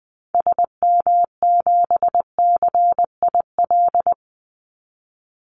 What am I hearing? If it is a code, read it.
Morse code, heard as a keyed tone, about 19 wpm, 700 Hz. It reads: SM7CIL
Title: Call Signs Repeated 2x - 20wpm